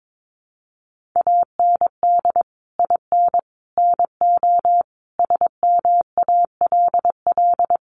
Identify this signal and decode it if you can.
Morse code, heard as a keyed tone, about 22 wpm, 700 Hz. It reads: AND IN NO SMALL
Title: Sentences from Top 1500 Words 22wpm